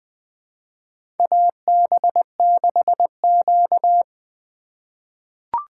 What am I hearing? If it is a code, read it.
Morse code, heard as a keyed tone, about 20 wpm, 700 Hz. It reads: AB6Q
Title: USA Call Signs - 2x Character Spacing - 20wpm